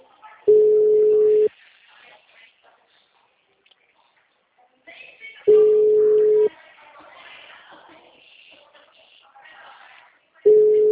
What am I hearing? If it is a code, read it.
Ringback tone (Europe, 425 Hz), with speech.